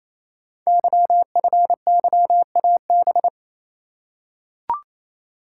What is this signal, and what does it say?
Morse code, heard as a keyed tone, about 27 wpm, 700 Hz. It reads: YFYAB
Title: Code Groups - 5 Letter:Numbers - 28wpm